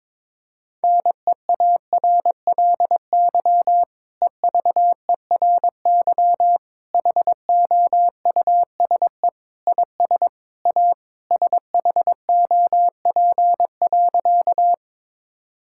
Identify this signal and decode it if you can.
Morse code, heard as a keyed tone, about 22 wpm, 700 Hz. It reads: NEARLY EVERY HOUSE IS A SHOP.
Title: Sentences from Top 1600 Words 22wpm